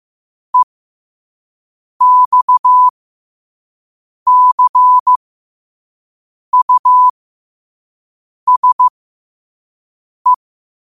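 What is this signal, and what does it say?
Morse code, heard as a keyed tone, about 15 wpm, spaced as for 5 wpm, 1000 Hz. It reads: EXCUSE
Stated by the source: Morse practice at 5 wpm.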